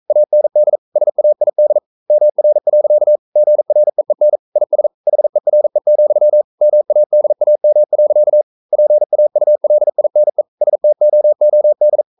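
Morse code, heard as a keyed tone, about 42 wpm, 600 Hz. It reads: AND SAID MR. GREEN IS HERE, MADAM. PAULINE STOOD
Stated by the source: Morse code practice at 41 wpm, from CW med Gnister.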